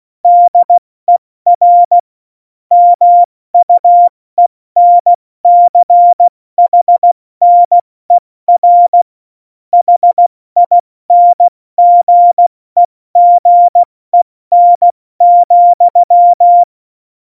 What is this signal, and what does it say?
Morse code, heard as a keyed tone, about 16 wpm, 700 Hz. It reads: DER MUENCHNER HINGEGEN,
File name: Morsekurs / Witze / Witz 08. Oktober 2019, 70 F80